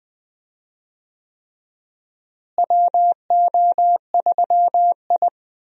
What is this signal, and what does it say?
Morse code, heard as a keyed tone, about 20 wpm, 700 Hz. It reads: WO3I
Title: USA Call Signs - 4 Characters - Repeated 3x - 20wpm